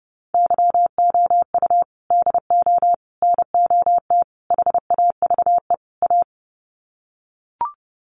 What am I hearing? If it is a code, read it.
Morse code, heard as a keyed tone, about 30 wpm, 700 Hz. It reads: YOU DO NOT HAVE A
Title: Sets of 5 Words 30wpm